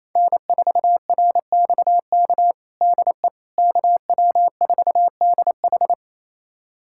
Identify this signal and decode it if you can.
Morse code, heard as a keyed tone, about 27 wpm, 700 Hz. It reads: N4RXK DE KW4DH
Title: USA Callsign DE Callsign - 5 Characters - 28wpm